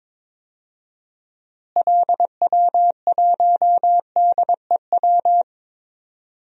Morse code, heard as a keyed tone, about 22 wpm, 700 Hz. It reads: LW1DEW
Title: DXCC-100 Call Signs - 6 Characters - Repeated 2x - 22wpm